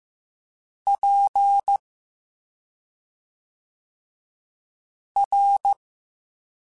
Morse code, heard as a keyed tone, about 15 wpm, spaced as for 2 wpm, 800 Hz. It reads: PR